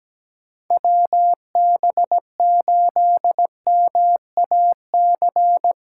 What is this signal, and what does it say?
Morse code, heard as a keyed tone, about 17 wpm, 700 Hz. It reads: WB8MAC
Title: USA Callsign DE Callsign 17wpm